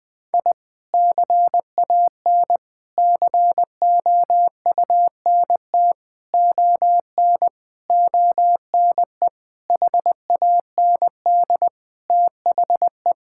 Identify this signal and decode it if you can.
Morse code, heard as a keyed tone, about 20 wpm, 700 Hz. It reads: I CAN COUNT ON ONE HAND THE
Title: Sentences with 9-Word Phrases - Spoken Beginning - 20wpm